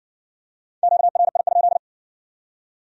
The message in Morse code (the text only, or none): QRI?